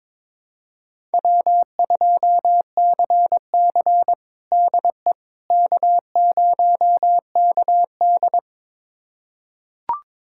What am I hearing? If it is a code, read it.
Morse code, heard as a keyed tone, about 22 wpm, 700 Hz. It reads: W2CC DE K0KD
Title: USA Callsign DE Callsign - 4 Characters - 22wpm